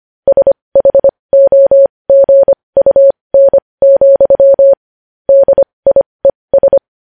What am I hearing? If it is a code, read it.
Morse code, heard as a keyed tone, about 25 wpm, spaced as for 20 wpm, 550 Hz. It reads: SHOGUN, DIES
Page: news-headlines-in-morse-code-at-20-wpm